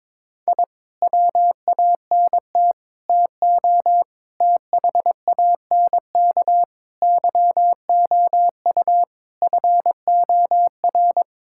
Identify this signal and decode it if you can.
Morse code, heard as a keyed tone, about 22 wpm, 700 Hz. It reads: I WANT TO THANK YOU FOR
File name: Sets of 9 Words - 22wpm